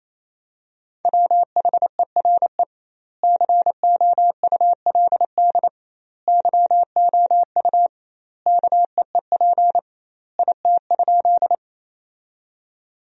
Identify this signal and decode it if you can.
Morse code, heard as a keyed tone, about 27 wpm, 700 Hz. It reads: WHERE COULD YOU KEEP IT?
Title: 2x Word-Spacing - Sentences from Top 300 Words - 28wpm